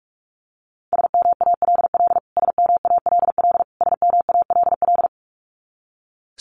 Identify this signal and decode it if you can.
Morse code, heard as a keyed tone, about 44 wpm, 700 Hz. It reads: SMALL SMALL SMALL
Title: Top 500 Words Repeated 45wpm